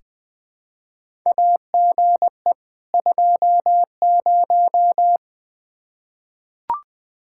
Morse code, heard as a keyed tone, about 20 wpm, 700 Hz. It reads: AGE 20
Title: Practice QSO - Exchange 3 - More Info - 20wpm